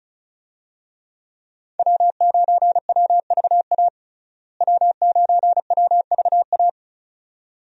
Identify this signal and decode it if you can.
Morse code, heard as a keyed tone, about 34 wpm, 700 Hz. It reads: W9WVA W9WVA
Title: USA Call Signs - 5 Characters - Repeated 2x - 35wpm